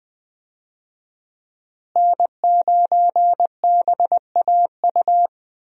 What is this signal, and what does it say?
Morse code, heard as a keyed tone, about 20 wpm, 700 Hz. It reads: N9BAU
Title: USA Call Signs - 5 Characters - Repeated 3x - 20wpm